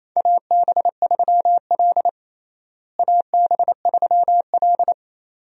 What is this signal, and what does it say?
Morse code, heard as a keyed tone, about 28 wpm, 700 Hz. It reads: AB3L AB3L
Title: USA Call Signs - 4 Characters - Repeated 2x - 28wpm